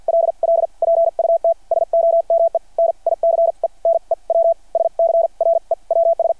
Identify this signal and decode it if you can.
Morse code, heard as a keyed tone, about 51 wpm, spaced as for 41 wpm, 660 Hz. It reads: LL PUT SOME NICE NEW SCREWS
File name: archived / CRSnet